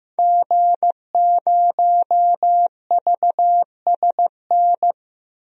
Morse code, heard as a keyed tone, about 15 wpm, 700 Hz. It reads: G0VSN
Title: DXCC-100 Call Signs - 5 Characters - 15wpm